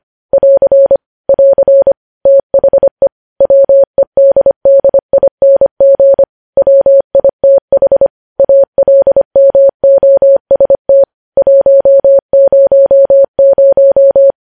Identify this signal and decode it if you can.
Morse code, heard as a keyed tone, about 25 wpm, 550 Hz. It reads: <AR> <AR> THE WEDDING WITH ALMOST 100
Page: news headlines in morse code at 25 wpm